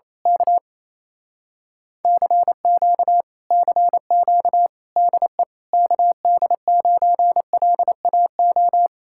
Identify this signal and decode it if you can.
Morse code, heard as a keyed tone, about 27 wpm, 700 Hz. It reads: K CQ CQ DE KD9LAO
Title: Speed-Racing Common QSO Phrases - 28wpm